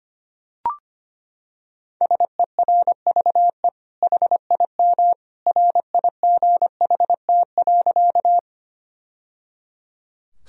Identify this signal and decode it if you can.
Morse code, heard as a keyed tone, about 25 wpm, 700 Hz. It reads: SERVE HIM RIGHT.
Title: Sentences from Top 1000 Words - Review Part 2 - 25wpm